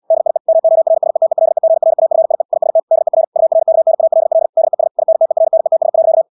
Morse code, heard as a keyed tone, about 74 wpm, 650 Hz. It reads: DIE MORSETELEGRAFIE IST DER URKNALL DES INTERNETS.
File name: dk5ke / Motto 375 5BpM